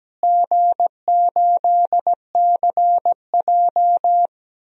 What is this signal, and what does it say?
Morse code, heard as a keyed tone, about 17 wpm, 700 Hz. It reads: G8CJ
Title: DXCC-100 Call Signs - 4 Characters - Repeated 2x - 17wpm